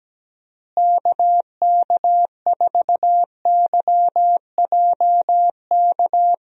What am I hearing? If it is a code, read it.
Morse code, heard as a keyed tone, about 17 wpm, 700 Hz. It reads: KK4YJK
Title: USA Call Signs - 6 Characters - Repeated 2x - 17wpm